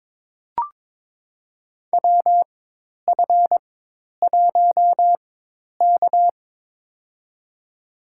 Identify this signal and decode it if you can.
Morse code, heard as a keyed tone, about 22 wpm, spaced as for 10 wpm, 700 Hz. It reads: WF1K
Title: USA Call Signs - 4x Character Spacing - 22wpm